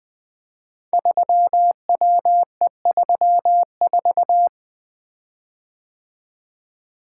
Morse code, heard as a keyed tone, about 21 wpm, 700 Hz. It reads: 3WE34
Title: Code Groups - 5 Letter:Numbers - 20wpm